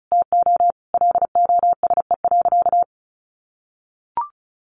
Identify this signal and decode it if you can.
Morse code, heard as a keyed tone, about 34 wpm, 700 Hz. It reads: TO LOSE.
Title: Sentences from Top 1100 Words 35wpm